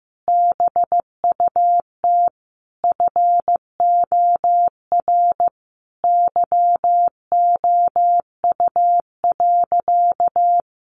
Morse code, heard as a keyed tone, about 15 wpm, 700 Hz. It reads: BUT FOR YOU.
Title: Sentences from Top 1300 Words 15wpm